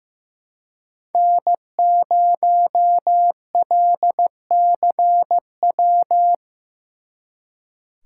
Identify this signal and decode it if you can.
Morse code, heard as a keyed tone, about 17 wpm, 700 Hz. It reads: N0LCW
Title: Speed Racing - USA Call Signs 15wpm